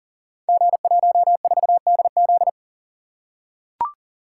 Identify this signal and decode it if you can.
Morse code, heard as a keyed tone, about 39 wpm, 700 Hz. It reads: G14DZ
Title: Code Groups - 5 Letter:Numbers - 40wpm